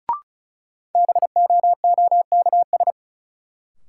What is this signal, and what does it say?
Morse code, heard as a keyed tone, about 36 wpm, 700 Hz. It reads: BOOKS
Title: Sentences from Top 1200 Words - New Word Drill Alt - 35wpm